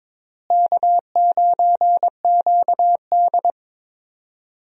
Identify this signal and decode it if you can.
Morse code, heard as a keyed tone, about 22 wpm, 700 Hz. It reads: K9QD
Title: USA Call Signs - 4 Characters - Repeated 2x - 22wpm